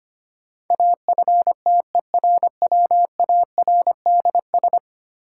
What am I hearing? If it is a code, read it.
Morse code, heard as a keyed tone, about 25 wpm, 700 Hz. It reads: AFTERWARDS
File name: 10 Letter Words - First 3 Letters Given - 25wpm